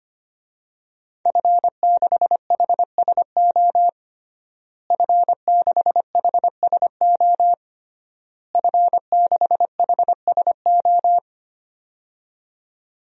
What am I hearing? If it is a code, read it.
Morse code, heard as a keyed tone, about 25 wpm, 700 Hz. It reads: F6HSO F6HSO F6HSO
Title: DXCC-100 Call Signs - 5 Characters - Repeated 3x - 25wpm